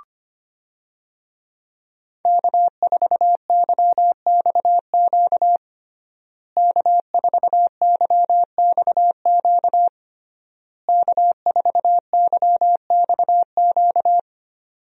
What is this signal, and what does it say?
Morse code, heard as a keyed tone, about 25 wpm, 700 Hz. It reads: K4YXQ K4YXQ K4YXQ
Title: USA Call Signs - 5 Characters - Repeated 3x - 25wpm